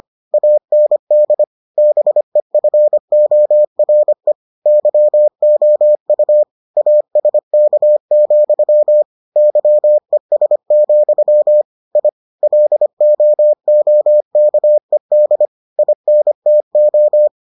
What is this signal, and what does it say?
Morse code, heard as a keyed tone, about 25 wpm, 600 Hz. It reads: AND BEFORE YOU ASK, YES, I LOOKED INTO